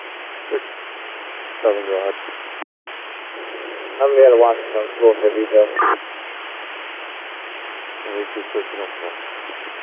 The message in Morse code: none

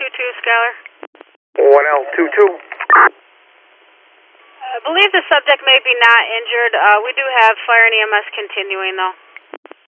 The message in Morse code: none

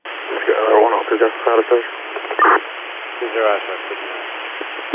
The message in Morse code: none

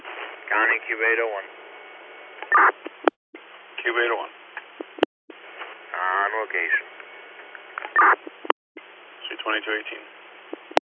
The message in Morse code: none

